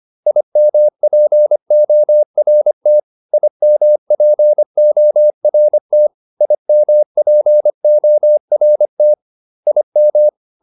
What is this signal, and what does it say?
Morse code, heard as a keyed tone, about 25 wpm, 600 Hz. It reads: IMPORT IMPORT IMPORT IM